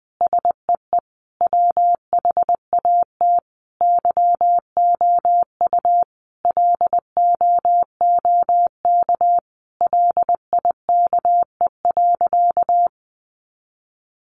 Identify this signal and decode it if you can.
Morse code, heard as a keyed tone, about 20 wpm, 700 Hz. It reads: SEE WHAT YOU LOOK LIKE.
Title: Sentences from Top 1300 Words 20wpm